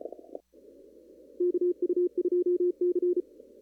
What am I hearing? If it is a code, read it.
Morse code, heard as a keyed tone, about 34 wpm, 360 Hz. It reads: KU2C
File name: CW / kt5j wpx cw-op-k5pi